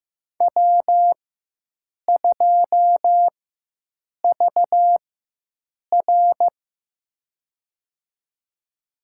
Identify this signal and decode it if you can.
Morse code, heard as a keyed tone, about 15 wpm, spaced as for 7 wpm, 700 Hz. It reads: W2VR
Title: USA Call Signs - 4x Character Spacing - 15wpm